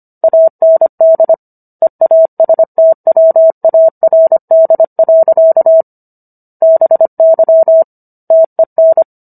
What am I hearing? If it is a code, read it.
Morse code, heard as a keyed tone, about 25 wpm, 650 Hz. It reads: AND EASTWARD. BY TEN